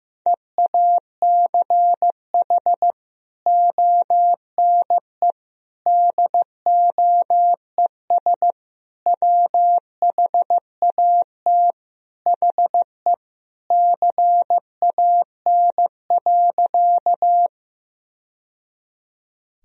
Morse code, heard as a keyed tone, about 15 wpm, 700 Hz. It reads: EACH ONE DOES WHAT HE CAN.
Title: Sentences from Top 1000 Words - Review Part 1 - 15wpm